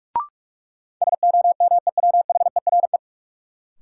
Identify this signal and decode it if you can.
Morse code, heard as a keyed tone, about 46 wpm, 700 Hz. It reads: SOMEWHERE
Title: Sentences from Top 1400 Words - New Word Drill - 45wpm